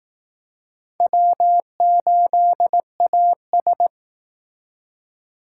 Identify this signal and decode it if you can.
Morse code, heard as a keyed tone, about 18 wpm, 700 Hz. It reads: W8AS
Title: Speed Racing - USA 1x2 Call Signs 15wpm